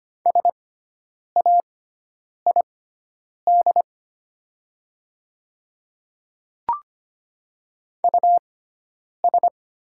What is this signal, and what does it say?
Morse code, heard as a keyed tone, about 25 wpm, spaced as for 9 wpm, 690 Hz. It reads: SAID US
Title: 6x Character Spacing - Top 200 Words - 25wpm